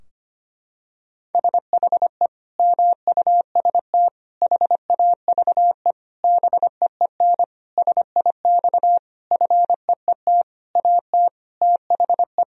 Morse code, heard as a keyed tone, about 25 wpm, 700 Hz. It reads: SHE MUST HAVE BEEN SIX FEET AT THE
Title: Sentences from Top 1000 Words - Review Part 2 - 25wpm